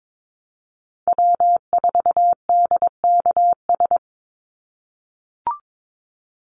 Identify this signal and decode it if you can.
Morse code, heard as a keyed tone, about 22 wpm, 700 Hz. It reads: W4DKS
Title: Call Signs - 22wpm